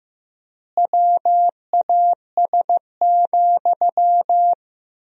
Morse code, heard as a keyed tone, about 15 wpm, 700 Hz. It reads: WAS,
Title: 3x Word-Spacing - Sentences from Top 400 Words - 15wpm